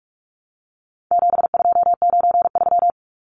Morse code, heard as a keyed tone, about 44 wpm, 700 Hz. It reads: 7293
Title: Numbers - 4-Digit 45wpm